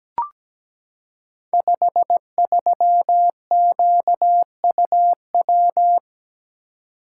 Morse code, heard as a keyed tone, about 17 wpm, 700 Hz. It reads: 53QUW
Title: Code Groups - 5 Letter:Numbers - 17wpm